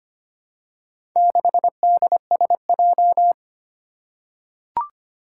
Morse code, heard as a keyed tone, about 25 wpm, 700 Hz. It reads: 6DSJ
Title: Four Challenging Letter-Number Combinations 25wpm